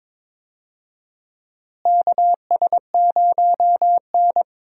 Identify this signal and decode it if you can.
Morse code, heard as a keyed tone, about 22 wpm, 700 Hz. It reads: KS0N
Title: USA Call Signs - 4 Characters - Repeated 2x - 22wpm